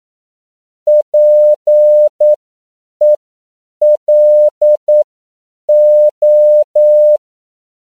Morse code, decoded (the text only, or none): PELO